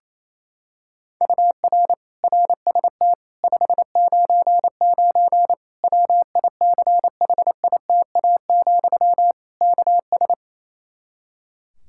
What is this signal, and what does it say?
Morse code, heard as a keyed tone, about 28 wpm, 700 Hz. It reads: UR RST 599 WICHITA, KS
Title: Practice QSO - Exchange 1 - Beginning - 28wpm